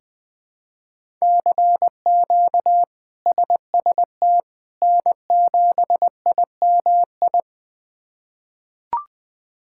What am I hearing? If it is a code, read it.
Morse code, heard as a keyed tone, about 20 wpm, 700 Hz. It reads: CQ SST N7IMI
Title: K1USN SST QSO Practice - 20wpm